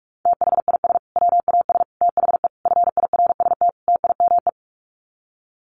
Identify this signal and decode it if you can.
Morse code, heard as a keyed tone, about 44 wpm, 700 Hz. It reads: THIS WAS THE FIRST TIME
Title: Sets of 5 Words Encore 45wpm